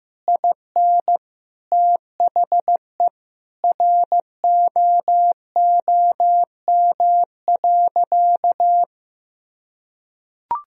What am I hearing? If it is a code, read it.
Morse code, heard as a keyed tone, about 16 wpm, 700 Hz. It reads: IN THE ROOM.
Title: Sentences from Top 1000 Words - Review Part 4 - 15wpm